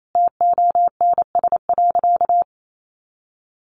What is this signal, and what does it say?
Morse code, heard as a keyed tone, about 28 wpm, 700 Hz. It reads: TONS.
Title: A Sound Of Thunder 28wpm